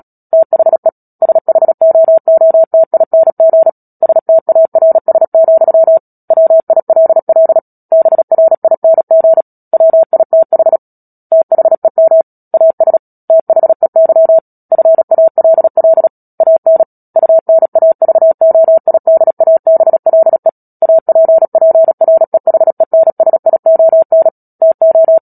Morse code, heard as a keyed tone, about 37 wpm, 650 Hz. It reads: THE SHOOTING STARS, WILL BRING WITH THEM AS THEY FALL AN UNAVOIDABLE APPREHENSION TO
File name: Book Two / 10-Chapter 10 - Epilogue - 36 WPM